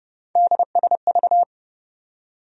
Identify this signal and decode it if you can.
Morse code, heard as a keyed tone, about 30 wpm, 700 Hz. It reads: DSV
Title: USA Call Sign Postfixes - 3 Letters 30wpm